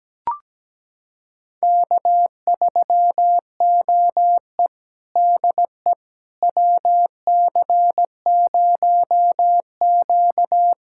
Morse code, heard as a keyed tone, about 17 wpm, 700 Hz. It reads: K3OE DE WC0Q
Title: USA Callsign DE Callsign - 4 Characters - 17wpm